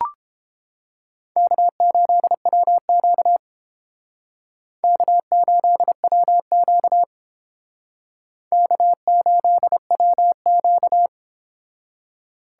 Morse code, heard as a keyed tone, about 31 wpm, 700 Hz. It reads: K8WQ K8WQ K8WQ